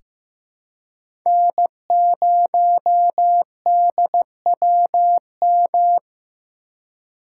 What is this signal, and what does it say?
Morse code, heard as a keyed tone, about 15 wpm, 700 Hz. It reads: N0DWM
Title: USA Call Signs - 5 Characters - 3x Character Spacing - 15wpm